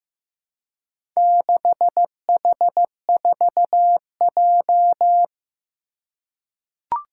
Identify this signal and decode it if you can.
Morse code, heard as a keyed tone, about 15 wpm, 700 Hz. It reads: 6H4J
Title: Four Challenging Letter-Number Combinations 15wpm